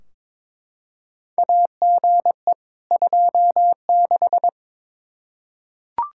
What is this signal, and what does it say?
Morse code, heard as a keyed tone, about 22 wpm, 700 Hz. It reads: AGE 26
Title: Practice QSO - Exchange 3 - More Info - 22wpm